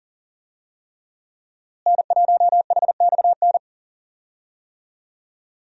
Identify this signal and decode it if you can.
Morse code, heard as a keyed tone, about 40 wpm, 700 Hz. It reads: N1HXN
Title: USA Call Signs - 5 Characters - Repeated 3x - 40wpm